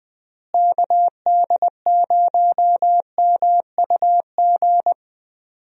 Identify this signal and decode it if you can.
Morse code, heard as a keyed tone, about 21 wpm, 700 Hz. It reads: KD0MUG
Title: USA Call Signs - 6 Characters - Repeated 3x - 20wpm